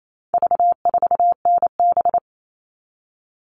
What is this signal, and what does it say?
Morse code, heard as a keyed tone, about 28 wpm, 700 Hz. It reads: V4NB
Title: All Cut Numbers - 4-Digit 28wpm